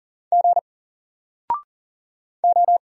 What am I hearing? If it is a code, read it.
Morse code, heard as a keyed tone, about 40 wpm, 700 Hz. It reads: G O
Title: Single Letter-Number 40wpm